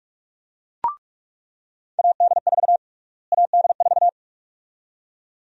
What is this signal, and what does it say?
Morse code, heard as a keyed tone, about 45 wpm, 700 Hz. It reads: AD4 AD4
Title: USA Call Sign Prefixes - Repeated 2x - 45wpm